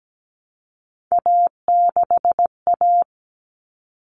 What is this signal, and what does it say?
Morse code, heard as a keyed tone, about 17 wpm, 700 Hz. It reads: A6A
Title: Common Cut Numbers - 3-Digit 17wpm